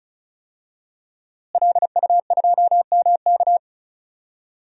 Morse code, heard as a keyed tone, about 36 wpm, 700 Hz. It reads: LU2MK
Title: DXCC-100 Call Signs - 5 Characters - Repeated 3x - 35wpm